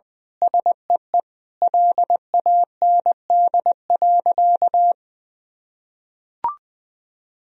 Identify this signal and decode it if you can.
Morse code, heard as a keyed tone, about 20 wpm, 700 Hz. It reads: SEE LAND.
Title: Sentences from Top 1000 Words - Review Part 2 - 20wpm